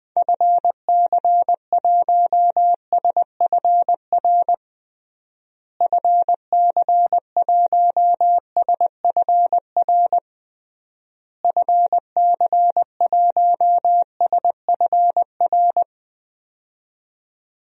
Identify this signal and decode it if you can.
Morse code, heard as a keyed tone, about 21 wpm, 700 Hz. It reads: FC1SFR FC1SFR FC1SFR
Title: DXCC-100 Call Signs - 6 Characters - Repeated 3x - 20wpm